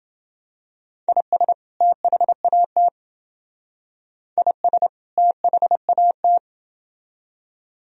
Morse code, heard as a keyed tone, about 28 wpm, 700 Hz. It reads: IS THAT IS THAT